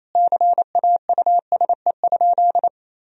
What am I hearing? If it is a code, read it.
Morse code, heard as a keyed tone, about 28 wpm, 700 Hz. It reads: CAUSE?
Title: Sentences from Top 1000 Words - Review Part 1 - 28wpm